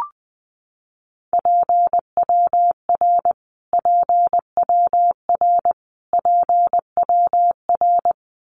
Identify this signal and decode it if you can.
Morse code, heard as a keyed tone, about 20 wpm, 700 Hz. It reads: PWR PWR PWR
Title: Abbreviations Repeated 3x - 20wpm